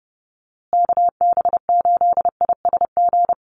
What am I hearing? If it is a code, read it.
Morse code, heard as a keyed tone, about 30 wpm, 700 Hz. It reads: KB8ISG
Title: USA Call Signs Repeated 2x 30wpm